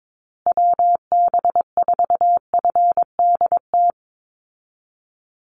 Morse code, heard as a keyed tone, about 22 wpm, 700 Hz. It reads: WB4FDT
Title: Call Signs Repeated 3x - 22wpm